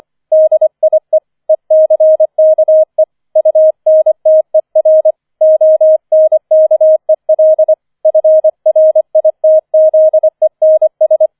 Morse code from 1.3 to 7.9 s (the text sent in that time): ECKE UNTER ONKEL